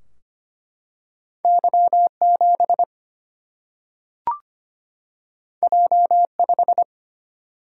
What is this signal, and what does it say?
Morse code, heard as a keyed tone, about 25 wpm, 700 Hz. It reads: Y7 J5
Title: Single-Letter Single-Number 25wpm